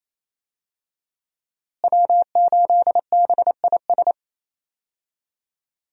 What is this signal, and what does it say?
Morse code, heard as a keyed tone, about 28 wpm, 700 Hz. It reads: W8BIS
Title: USA Call Signs - 5 Characters - Repeated 3x - 28wpm